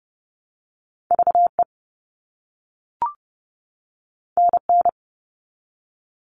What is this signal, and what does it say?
Morse code, heard as a keyed tone, about 30 wpm, 700 Hz. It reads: V E N N
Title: All Cut Numbers - 2-Digit 30wpm